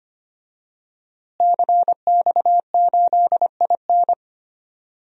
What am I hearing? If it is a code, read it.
Morse code, heard as a keyed tone, about 25 wpm, 700 Hz. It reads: CX8IN
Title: DXCC-100 Call Signs - 5 Characters - Repeated 2x - 25wpm